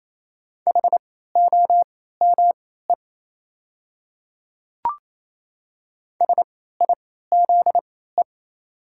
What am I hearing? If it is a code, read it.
Morse code, heard as a keyed tone, about 27 wpm, spaced as for 16 wpm, 690 Hz. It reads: HOME SIZE
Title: 3x Character Spacing - Top 500 Words - 28wpm